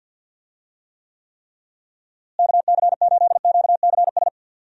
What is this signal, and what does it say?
Morse code, heard as a keyed tone, about 50 wpm, 700 Hz. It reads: KC8QCS
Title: USA Call Signs - 6 Characters - Repeated 3x - 50wpm